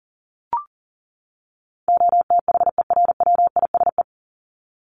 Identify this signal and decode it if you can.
Morse code, heard as a keyed tone, about 40 wpm, 700 Hz. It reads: OTHERWISE
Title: Top 1001-1250 Words 40wpm